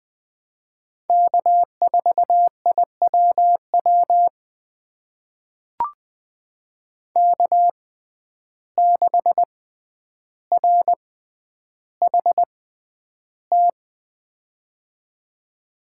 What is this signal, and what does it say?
Morse code, heard as a keyed tone, about 20 wpm, spaced as for 10 wpm, 700 Hz. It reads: K4IWW K6RHT
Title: USA Call Signs - 5 Characters - 6x Character Spacing - 20wpm